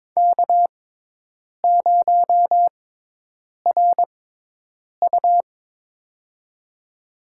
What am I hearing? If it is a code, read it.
Morse code, heard as a keyed tone, about 22 wpm, spaced as for 8 wpm, 700 Hz. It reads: K0RU
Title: USA Call Signs - 6x Character Spacing - 22wpm